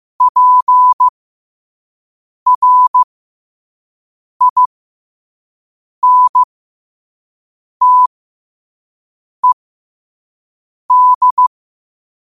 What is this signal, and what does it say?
Morse code, heard as a keyed tone, about 15 wpm, spaced as for 5 wpm, 1000 Hz. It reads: PRINTED